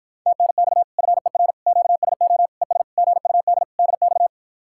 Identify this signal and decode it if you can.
Morse code, heard as a keyed tone, about 54 wpm, 700 Hz. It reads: TNX FER QSO ES GUD DX